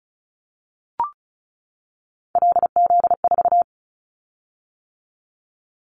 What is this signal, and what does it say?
Morse code, heard as a keyed tone, about 34 wpm, 700 Hz. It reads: LZ4
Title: Callsign Prefixes for Top 25 DXCC - 35wpm